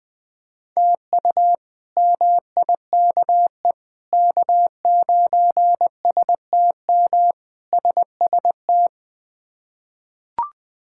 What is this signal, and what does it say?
Morse code, heard as a keyed tone, about 21 wpm, 700 Hz. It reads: TU MIKE K9STM SST
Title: K1USN SST QSO Practice - 20wpm